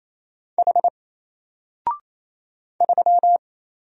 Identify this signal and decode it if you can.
Morse code, heard as a keyed tone, about 27 wpm, 700 Hz. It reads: H 3
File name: Single-Letter-Number-28wpm